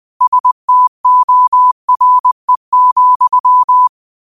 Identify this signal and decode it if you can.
Morse code, heard as a keyed tone, about 20 wpm, 1000 Hz. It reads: STORE,
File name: qotdfiles / 20wpm